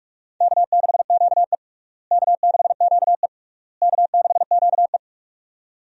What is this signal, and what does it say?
Morse code, heard as a keyed tone, about 42 wpm, 700 Hz. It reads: K6QE K6QE K6QE
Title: USA Call Signs - 4 Characters - Repeated 3x - 45wpm